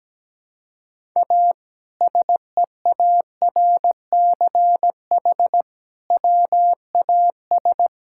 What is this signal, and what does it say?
Morse code, heard as a keyed tone, about 17 wpm, 700 Hz. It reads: A SEARCH WAS
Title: Sentences from Top 1000 Words - Review Part 5 - 17wpm